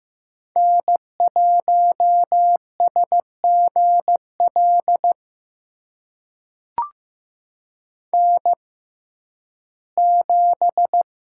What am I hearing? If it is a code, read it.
Morse code, heard as a keyed tone, about 15 wpm, spaced as for 11 wpm, 700 Hz. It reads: N1SGL N7
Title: USA Call Signs - 5 Characters - 6x Character Spacing - 15wpm